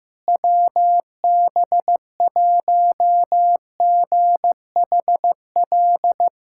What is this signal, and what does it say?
Morse code, heard as a keyed tone, about 15 wpm, 700 Hz. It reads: WB1GHL
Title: USA Callsign DE Callsign - 6 Characters - 15wpm